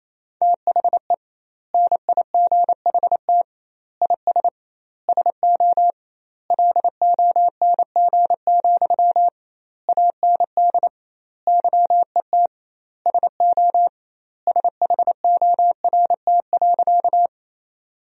Morse code, heard as a keyed tone, about 28 wpm, 700 Hz. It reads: THE NIGHT IS SO LONG, AND YET SO SHORT.
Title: 2x Word-Spacing - Sentences from Top 500 Words - 28wpm